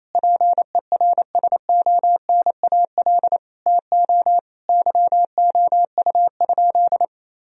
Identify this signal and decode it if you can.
Morse code, heard as a keyed tone, about 28 wpm, 700 Hz. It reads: PERSONAL TO YOU?
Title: Sentences from Top 1000 Words - Review Part 4 - 28wpm